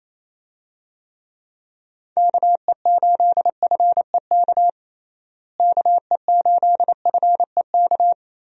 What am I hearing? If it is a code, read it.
Morse code, heard as a keyed tone, about 28 wpm, 700 Hz. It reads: KE8FEK KE8FEK
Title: USA Call Signs - 6 Characters - Repeated 3x - 28wpm